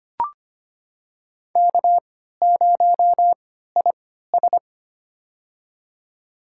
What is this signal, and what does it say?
Morse code, heard as a keyed tone, about 25 wpm, spaced as for 14 wpm, 700 Hz. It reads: K0IS
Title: USA Call Signs - 3x Character Spacing - 25wpm